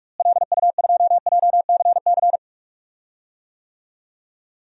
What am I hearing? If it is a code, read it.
Morse code, heard as a keyed tone, about 46 wpm, 700 Hz. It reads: LU2JCC
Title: DXCC-100 Call Signs - 6 Characters - Repeated 2x - 45wpm